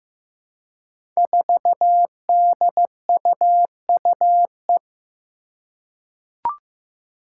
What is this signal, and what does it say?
Morse code, heard as a keyed tone, about 15 wpm, 700 Hz. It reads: 4DUUE
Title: Five Challenging Letter-Number Combinations 15wpm